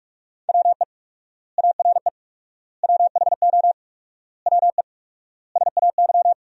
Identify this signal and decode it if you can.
Morse code, heard as a keyed tone, about 45 wpm, 700 Hz. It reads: WE ARE WHO WE SAY
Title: 4x Word-Spacing - Sentences from Top 200 Words - 45wpm